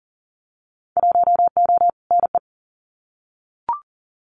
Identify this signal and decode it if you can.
Morse code, heard as a keyed tone, about 40 wpm, 700 Hz. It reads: 1O NE
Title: Winter Field Day Exchanges Encore 40wpm